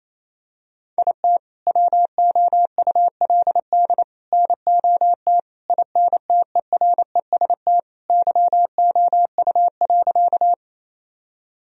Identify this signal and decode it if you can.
Morse code, heard as a keyed tone, about 28 wpm, 700 Hz. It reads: IT WOULD NOT INTEREST YOU.
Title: Sentences from Top 1000 Words - Review Part 2 - 28wpm